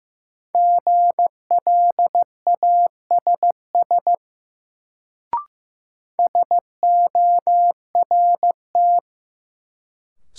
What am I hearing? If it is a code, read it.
Morse code, heard as a keyed tone, about 16 wpm, 700 Hz. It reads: GLASS SORT
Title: Sentences from Top 1000 Words - Review Part 3 - New Word Drill - 15wpm